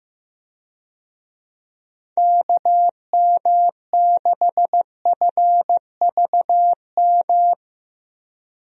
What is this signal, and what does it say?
Morse code, heard as a keyed tone, about 16 wpm, 700 Hz. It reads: KM6FVM
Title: USA Call Signs - 6 Characters - Repeated 2x - 15wpm